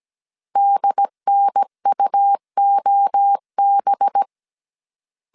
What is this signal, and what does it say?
Morse code, heard as a keyed tone, about 17 wpm, 800 Hz. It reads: DNUOB
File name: 16wpm